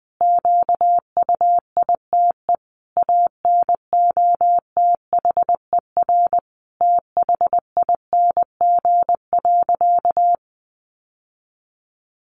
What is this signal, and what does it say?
Morse code, heard as a keyed tone, about 20 wpm, 700 Hz. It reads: QUITE ANOTHER THING.
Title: Sentences from Top 700 Words 20wpm